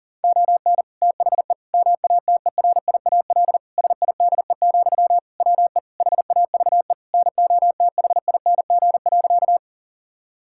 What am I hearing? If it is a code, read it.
Morse code, heard as a keyed tone, about 40 wpm, 700 Hz. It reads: ON THE MATERIAL SIDE, WE HAVE NOTHING.
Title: Sentences from Top 1000 Words - Review Part 5 - 40wpm